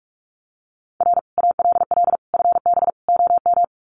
Morse code, heard as a keyed tone, about 45 wpm, 700 Hz. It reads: R ALL FB OM
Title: Common QSO Phrases 45wpm